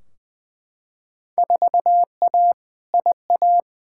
Morse code, heard as a keyed tone, about 20 wpm, 700 Hz. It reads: 4A IA
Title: Field Day Exchanges - 20wpm